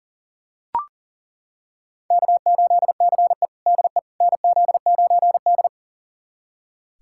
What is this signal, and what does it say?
Morse code, heard as a keyed tone, about 40 wpm, 700 Hz. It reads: K8CE DE NZ9D
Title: USA Callsign DE Callsign - 4 Characters - 40wpm